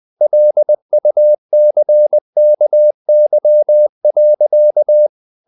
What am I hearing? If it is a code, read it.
Morse code, heard as a keyed tone, about 20 wpm, 600 Hz. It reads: LUCKY.
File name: the old man and the sea / the old man and the sea part 1 20wpm